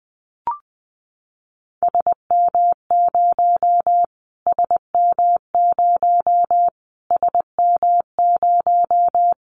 Morse code, heard as a keyed tone, about 20 wpm, 700 Hz. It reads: SM0 SM0 SM0
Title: Callsign Prefixes for Top 25 DXCC - Repeated 3x - 20wpm